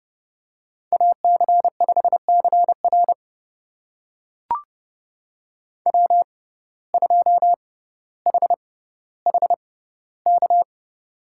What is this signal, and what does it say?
Morse code, heard as a keyed tone, about 30 wpm, spaced as for 15 wpm, 700 Hz. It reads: AC5CR W2HHK
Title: USA Call Signs - 5 Characters - 6x Character Spacing - 30wpm